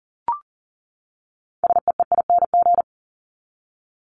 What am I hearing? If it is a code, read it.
Morse code, heard as a keyed tone, about 40 wpm, 700 Hz. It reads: SEEING
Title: ING Suffix - Top 100 Words with Base Words 40wpm